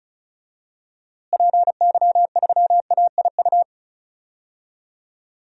Morse code, heard as a keyed tone, about 36 wpm, 700 Hz. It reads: PY3AIU
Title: DXCC-100 Call Signs - 6 Characters - Repeated 3x - 35wpm